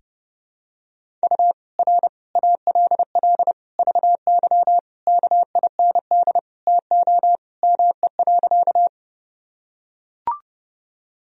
Morse code, heard as a keyed tone, about 30 wpm, 700 Hz. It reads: U R ALL VY KIND TO ME.
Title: Sentences from Top 200 Words and Abbreviations 30wpm